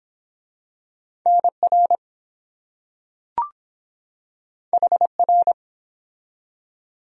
Morse code, heard as a keyed tone, about 23 wpm, 700 Hz. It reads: NR HR